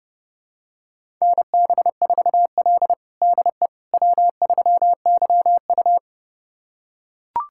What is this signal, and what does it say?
Morse code, heard as a keyed tone, about 30 wpm, 700 Hz. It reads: NB4L DE W3YU
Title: USA Callsign DE Callsign - 4 Characters - 30wpm